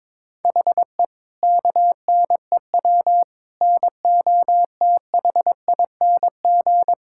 Morse code, heard as a keyed tone, about 22 wpm, 700 Hz. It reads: HE KNEW NOTHING
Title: Sentences from Top 1600 Words 22wpm